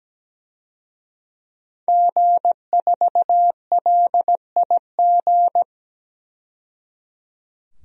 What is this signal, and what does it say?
Morse code, heard as a keyed tone, about 17 wpm, 700 Hz. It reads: G4LIG
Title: DXCC-100 Call Signs - 5 Characters - 17wpm